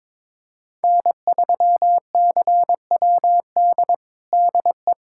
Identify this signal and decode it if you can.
Morse code, heard as a keyed tone, about 22 wpm, 700 Hz. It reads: N3CWD DE
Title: USA Callsign DE Callsign - 5 Characters - 22wpm